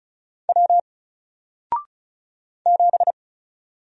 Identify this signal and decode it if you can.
Morse code, heard as a keyed tone, about 36 wpm, 700 Hz. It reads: W 7
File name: Single-Letter-Number-35wpm